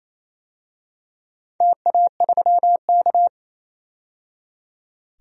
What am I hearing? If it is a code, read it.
Morse code, heard as a keyed tone, about 28 wpm, 700 Hz. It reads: TA3K
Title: DXCC-100 Call Signs - 4 Characters - 28wpm